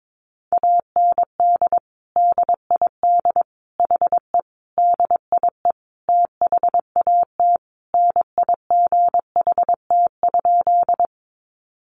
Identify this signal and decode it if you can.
Morse code, heard as a keyed tone, about 22 wpm, 700 Hz. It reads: AND DID HE DIE THAT NIGHT?
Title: Sentences from Top 1300 Words 22wpm